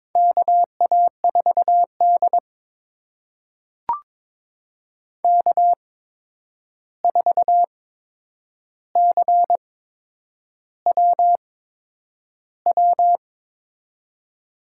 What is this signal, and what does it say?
Morse code, heard as a keyed tone, about 22 wpm, spaced as for 9 wpm, 700 Hz. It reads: KA4D K4CWW
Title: USA Call Signs - 8x Character Spacing - 22wpm